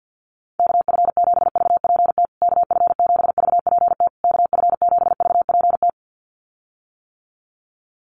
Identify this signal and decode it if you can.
Morse code, heard as a keyed tone, about 48 wpm, 700 Hz. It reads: KF7VPT KF7VPT KF7VPT
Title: USA Call Signs Repeated 3x 50wpm